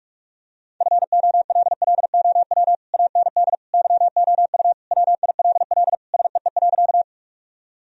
Morse code, heard as a keyed tone, about 42 wpm, 700 Hz. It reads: FOLLOW AND YOU WILL SEE.
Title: Sentences from Top 1000 Words - Review Part 1 - 45wpm